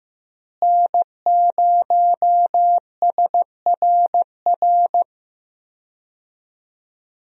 Morse code, heard as a keyed tone, about 15 wpm, 700 Hz. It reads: N0SRR
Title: USA Call Signs - 5 Characters - Repeated 2x - 15wpm